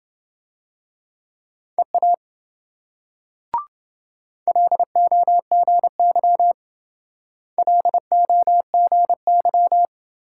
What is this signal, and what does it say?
Morse code, heard as a keyed tone, about 26 wpm, 700 Hz. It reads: EA LOGY LOGY